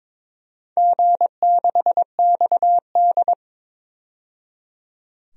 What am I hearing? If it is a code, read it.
Morse code, heard as a keyed tone, about 22 wpm, 700 Hz. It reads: G6XD
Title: DXCC-100 Call Signs - 4 Characters - 22wpm